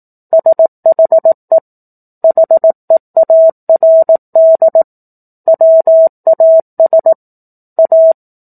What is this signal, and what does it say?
Morse code, heard as a keyed tone, about 18 wpm, 650 Hz. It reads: SHE HEARD WAS A
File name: 04-Chapter-04 18 WPM